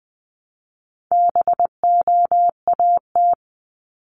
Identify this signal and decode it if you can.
Morse code, heard as a keyed tone, about 20 wpm, 700 Hz. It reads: BOAT
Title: Top 500 Words 20wpm